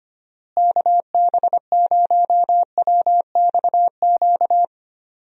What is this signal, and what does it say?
Morse code, heard as a keyed tone, about 25 wpm, 700 Hz. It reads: KB0WXQ
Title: USA Call Signs - 6 Characters - Repeated 2x - 25wpm